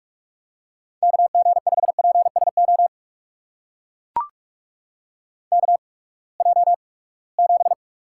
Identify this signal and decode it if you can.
Morse code, heard as a keyed tone, about 44 wpm, spaced as for 25 wpm, 700 Hz. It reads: KG5PSO KJ7
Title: USA Call Signs - 8x Character Spacing - 45wpm